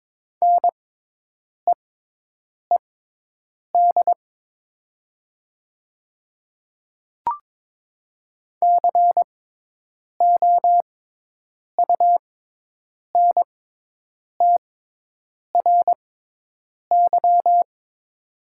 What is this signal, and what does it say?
Morse code, heard as a keyed tone, about 22 wpm, spaced as for 8 wpm, 700 Hz. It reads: NEED COUNTRY
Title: 6x Character Spacing - Top 500 Words - 22wpm